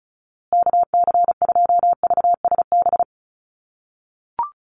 Morse code, heard as a keyed tone, about 34 wpm, 700 Hz. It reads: KC2VSB
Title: USA Call Signs Repeated 3x 35wpm